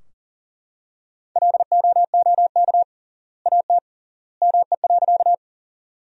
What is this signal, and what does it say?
Morse code, heard as a keyed tone, about 39 wpm, 700 Hz. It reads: LOOK AT ME.
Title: 3x Word-Spacing - Sentences from Top 200 Words - 40wpm